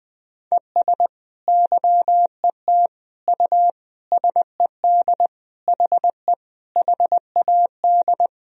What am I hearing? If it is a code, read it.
Morse code, heard as a keyed tone, about 20 wpm, 700 Hz. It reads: ES YET U SED HE HAD
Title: Sentences from Top 500 Words and Abbreviations 20wpm